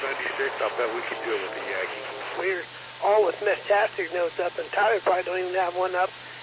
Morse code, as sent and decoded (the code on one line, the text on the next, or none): none